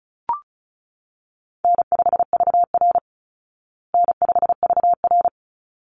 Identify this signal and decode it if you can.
Morse code, heard as a keyed tone, about 34 wpm, 700 Hz. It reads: N5VR N5VR
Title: Easy Call Signs Repeated 2x - 35wpm